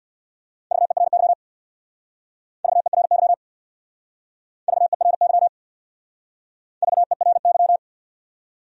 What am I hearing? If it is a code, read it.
Morse code, heard as a keyed tone, about 62 wpm, 700 Hz. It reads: VERY VERY VERY VERY
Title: Speed-Racing Top 300 Words 50wpm